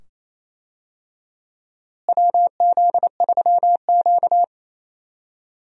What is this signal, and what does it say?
Morse code, heard as a keyed tone, about 28 wpm, 700 Hz. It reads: WZ3Q
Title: USA Call Signs - 4 Characters - Repeated 2x - 28wpm